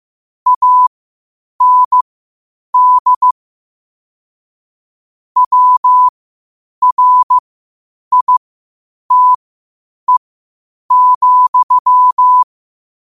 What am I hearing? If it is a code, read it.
Morse code, heard as a keyed tone, about 15 wpm, spaced as for 8 wpm, 1000 Hz. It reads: AND WRITE,
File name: 8wpm